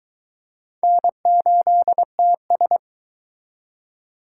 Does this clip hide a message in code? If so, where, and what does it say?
Yes: Morse code from 0.8 to 2.8 s, heard as a keyed tone, about 22 wpm, 700 Hz. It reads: N8TS